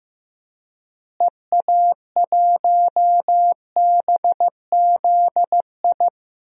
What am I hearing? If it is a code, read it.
Morse code, heard as a keyed tone, about 15 wpm, 700 Hz. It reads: EA1BZI
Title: DXCC-100 Call Signs - 6 Characters - Repeated 3x - 15wpm